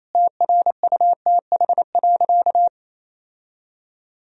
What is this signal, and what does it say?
Morse code, heard as a keyed tone, about 28 wpm, 700 Hz. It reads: TRUTH.
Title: Sentences from Top 1000 Words - Review Part 4 - 28wpm